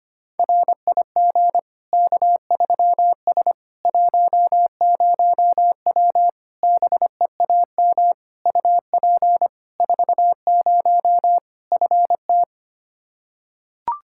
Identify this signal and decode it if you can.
Morse code, heard as a keyed tone, about 25 wpm, 700 Hz. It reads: RIG K3S 10W BEAM UP 40 FT
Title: Practice QSO - Exchange 2 - Info - 25wpm